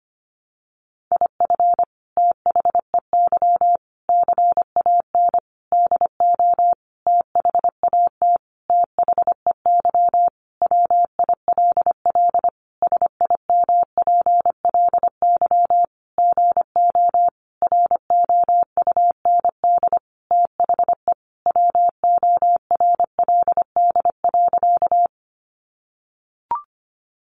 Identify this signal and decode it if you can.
Morse code, heard as a keyed tone, about 25 wpm, 700 Hz. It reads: IF THEY CAN DO THAT THEY WILL SIMPLY GO ROUND THE WORLD.
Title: Sentences from Top 600 Words 25wpm